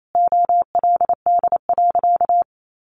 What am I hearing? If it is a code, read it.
Morse code, heard as a keyed tone, about 28 wpm, 700 Hz. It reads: OLD.
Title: Sentences from Top 1100 Words 28wpm